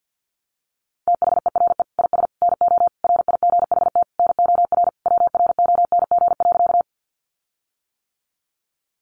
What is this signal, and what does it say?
Morse code, heard as a keyed tone, about 50 wpm, 700 Hz. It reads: THERE IS NO RIGHT NOR WRONG.
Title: Sentences from Top 1500 Words 50wpm